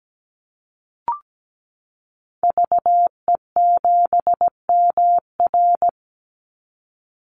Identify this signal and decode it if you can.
Morse code, heard as a keyed tone, about 17 wpm, 700 Hz. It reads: VE7MR
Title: Call Signs Repeated 2x - 17wpm